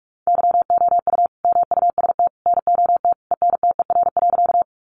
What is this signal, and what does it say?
Morse code, heard as a keyed tone, about 44 wpm, 700 Hz. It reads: YOU MUST NOT ENTER.
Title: Sentences from Top 1300 Words 45wpm